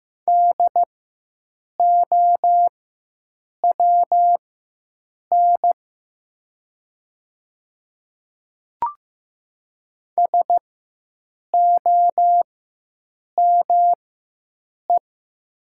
Morse code, heard as a keyed tone, about 15 wpm, spaced as for 7 wpm, 700 Hz. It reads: DOWN SOME
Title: 4x Character Spacing - Top 100 Words - 15wpm